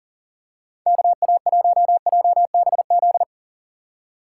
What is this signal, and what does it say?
Morse code, heard as a keyed tone, about 40 wpm, 700 Hz. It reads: KA1JBZ
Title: USA Call Signs - 6 Characters - 2x Character Spacing - 40wpm